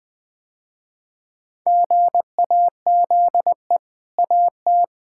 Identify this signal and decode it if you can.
Morse code, heard as a keyed tone, about 21 wpm, 700 Hz. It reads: GAZE AT
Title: Easy Sentences with Hints - 20wpm